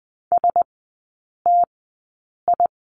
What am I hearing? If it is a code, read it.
Morse code, heard as a keyed tone, about 20 wpm, spaced as for 8 wpm, 700 Hz. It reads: STI
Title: 041 Lesson 8e - Number 1 - Three Characters 20wpm